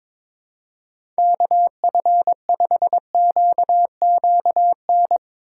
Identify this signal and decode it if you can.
Morse code, heard as a keyed tone, about 21 wpm, 700 Hz. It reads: KF5QQN
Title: USA Call Signs - 6 Characters - 2x Character Spacing - 22wpm